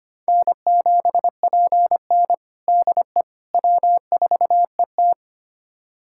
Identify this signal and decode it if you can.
Morse code, heard as a keyed tone, about 25 wpm, 700 Hz. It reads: N7PN DE W4ET
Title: USA Callsign DE Callsign - 4 Characters - 25wpm